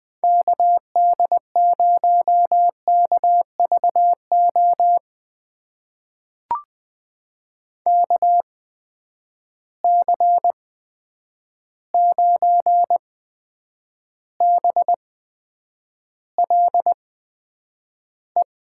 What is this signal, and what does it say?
Morse code, heard as a keyed tone, about 20 wpm, spaced as for 9 wpm, 700 Hz. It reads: KD0KVO KC9BLE
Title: USA Call Signs - 8x Character Spacing - 20wpm